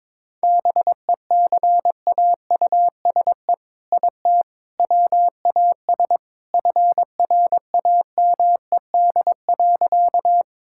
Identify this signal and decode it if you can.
Morse code, heard as a keyed tone, about 22 wpm, 700 Hz. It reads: BECAUSE IT WAS FRAMED.
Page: morse-code-ninja-practice-22wpm